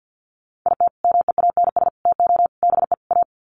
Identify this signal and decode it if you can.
Morse code, heard as a keyed tone, about 50 wpm, 700 Hz. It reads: IT MEANS TO BE A
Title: Sets of 5 Words Encore2 50wpm